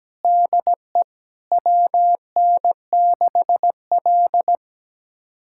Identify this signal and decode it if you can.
Morse code, heard as a keyed tone, about 17 wpm, 700 Hz. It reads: DE WN6L
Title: USA Callsign DE Callsign - 4 Characters - 17wpm